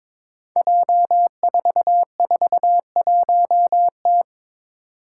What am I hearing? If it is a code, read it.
Morse code, heard as a keyed tone, about 22 wpm, 700 Hz. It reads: J441T
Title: Five Challenging Letter-Number Combinations 22wpm